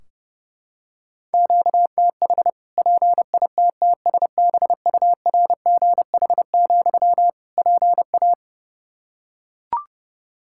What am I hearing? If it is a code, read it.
Morse code, heard as a keyed tone, about 30 wpm, 700 Hz. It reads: QTH PITTSBURGH, PA
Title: Practice QSO - Exchange 1 - Beginning - 30wpm